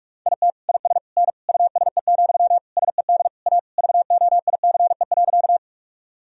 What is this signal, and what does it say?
Morse code, heard as a keyed tone, about 46 wpm, 700 Hz. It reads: IT IS N USE, SED A VOICE.
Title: Sentences from Top 500 Words and Abbreviations 45wpm